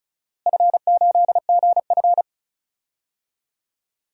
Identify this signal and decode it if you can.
Morse code, heard as a keyed tone, about 36 wpm, 700 Hz. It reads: F8GF
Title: DXCC-100 Call Signs - 4 Characters - Repeated 3x - 35wpm